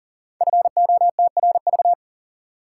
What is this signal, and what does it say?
Morse code, heard as a keyed tone, about 39 wpm, 700 Hz. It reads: FOTRV
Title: Code Groups - 5 Letters - 40wpm